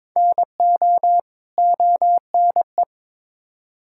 Morse code, heard as a keyed tone, about 22 wpm, 700 Hz. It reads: NO ONE
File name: Build A Phrase - 2-Word Phrases - 22wpm